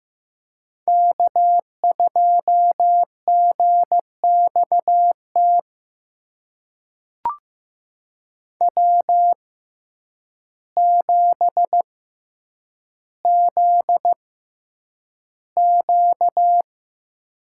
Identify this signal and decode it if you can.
Morse code, heard as a keyed tone, about 15 wpm, spaced as for 8 wpm, 700 Hz. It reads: K2GXT W7ZQ
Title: USA Call Signs - 6x Character Spacing - 15wpm